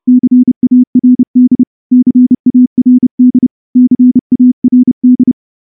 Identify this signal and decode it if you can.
Morse code, heard as a keyed tone, about 29 wpm, 250 Hz. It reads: CARD CARD CARD
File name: morse-training-30WPM